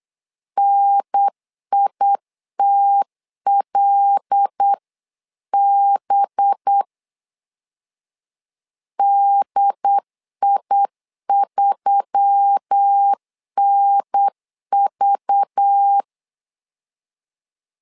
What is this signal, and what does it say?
Morse code, heard as a keyed tone, about 8 wpm, 800 Hz. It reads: NITLB DI3NV
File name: LEZIONI / 5 LBVC / 5 LBV 8wpm 2837